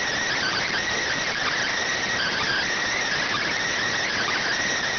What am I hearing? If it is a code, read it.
Emergency Alert System (SAME) tones, heard as harsh buzzing bursts. It is heard from 0.3 to 4.5 s.